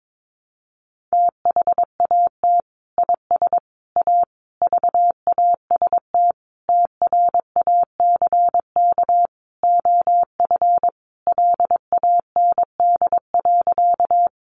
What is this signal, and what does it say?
Morse code, heard as a keyed tone, about 22 wpm, 700 Hz. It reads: THAT IS A VAST TRACK OF LAND.
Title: Sentences from Top 1300 Words 22wpm